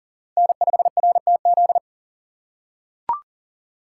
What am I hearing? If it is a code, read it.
Morse code, heard as a keyed tone, about 37 wpm, 700 Hz. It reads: N5RTZ
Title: USA Call Signs - 2x Character Spacing - 40wpm